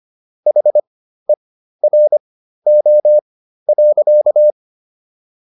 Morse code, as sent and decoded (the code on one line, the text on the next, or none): .... .-. --- .-.-.-
HERO.